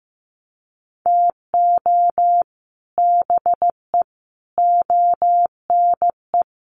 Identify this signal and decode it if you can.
Morse code, heard as a keyed tone, about 15 wpm, 700 Hz. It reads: TO BE ONE
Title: Sets of 5 Words 15wpm